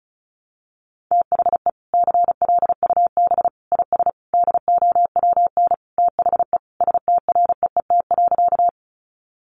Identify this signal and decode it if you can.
Morse code, heard as a keyed tone, about 34 wpm, 700 Hz. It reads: THE CLUB IS DOWN THE STREET.
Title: Sentences from Top 1200 Words 35wpm